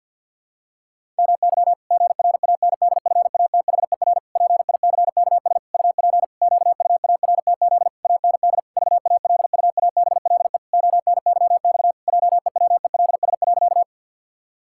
Morse code, heard as a keyed tone, about 49 wpm, 700 Hz. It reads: MY GRANDFATHER PICKS UP QUARTZ AND VALUABLE ONYX JEWELS.
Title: Pangram Challenge v2 - 50wpm